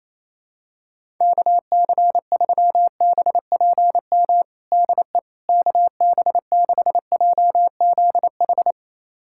Morse code, heard as a keyed tone, about 28 wpm, 700 Hz. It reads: KC3BPM DE KB6JZH
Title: USA Callsign DE Callsign - 6 Characters - 28wpm